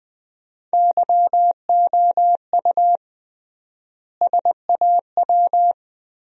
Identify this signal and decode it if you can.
Morse code, heard as a keyed tone, about 21 wpm, 700 Hz. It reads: YOU SAW
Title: 3x Word-Spacing - Sentences from Top 400 Words - 20wpm